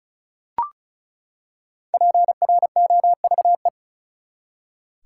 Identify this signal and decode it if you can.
Morse code, heard as a keyed tone, about 36 wpm, 700 Hz. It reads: PROVE
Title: Five Letter Words - 35wpm